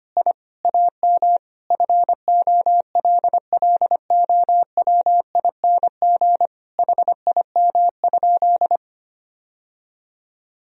Morse code, heard as a keyed tone, about 25 wpm, 700 Hz. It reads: I AM FOLLOWING HIM?
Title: Sentences from Top 1000 Words - Review Part 4 - 25wpm